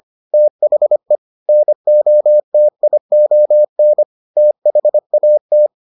Morse code, heard as a keyed tone, about 25 wpm, 600 Hz. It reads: THE NOTION THAT